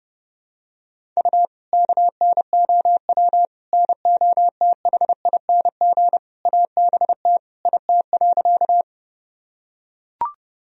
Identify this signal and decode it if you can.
Morse code, heard as a keyed tone, about 30 wpm, 700 Hz. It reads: U KNOW NOTHING ABT IT.
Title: Sentences from Top 500 Words and Abbreviations 30wpm